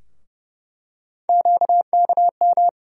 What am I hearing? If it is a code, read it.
Morse code, heard as a keyed tone, about 30 wpm, 700 Hz. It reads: QKM
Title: USA Call Sign Postfixes - 3 Letters 30wpm